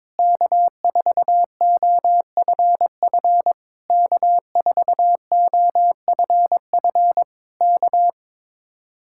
Morse code, heard as a keyed tone, about 22 wpm, 700 Hz. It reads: K4OFF K4OFF K
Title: Speed-Racing Common QSO Phrases - 22wpm